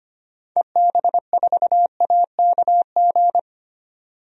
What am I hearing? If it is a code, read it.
Morse code, heard as a keyed tone, about 25 wpm, 700 Hz. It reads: EB4AKG
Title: DXCC-100 Call Signs - 6 Characters - 25wpm